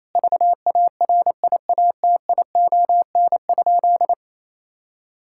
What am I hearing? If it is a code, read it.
Morse code, heard as a keyed tone, about 28 wpm, 700 Hz. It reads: VARIATION?
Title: Sentences from Top 1600 Words 28wpm